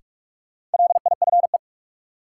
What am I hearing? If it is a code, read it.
Morse code, heard as a keyed tone, about 46 wpm, 700 Hz. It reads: LIFE
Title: Sets of 5 Words - New Word Drill - 45wpm